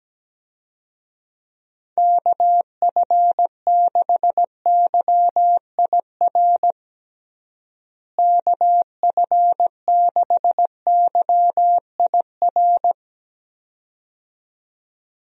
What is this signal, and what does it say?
Morse code, heard as a keyed tone, about 17 wpm, 700 Hz. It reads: KF6YIR KF6YIR
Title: USA Call Signs - 6 Characters - Repeated 2x - 17wpm